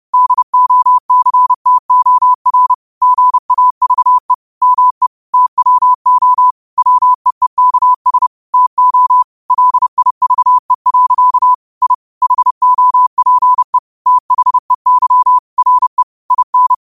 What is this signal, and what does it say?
Morse code, heard as a keyed tone, about 30 wpm, 1000 Hz. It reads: DOCTOR GAVE ME TWO WEEKS TO LIVE. I HOPE THEY RE IN
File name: qotdfiles / 30wpm